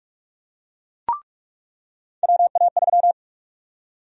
Morse code, heard as a keyed tone, about 46 wpm, 700 Hz. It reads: WA3
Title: USA Call Sign Prefixes - 45wpm